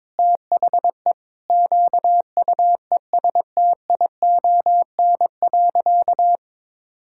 Morse code, heard as a keyed tone, about 22 wpm, 700 Hz. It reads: THE QUESTION.
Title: Sentences from Top 1000 Words - Review Part 5 - 22wpm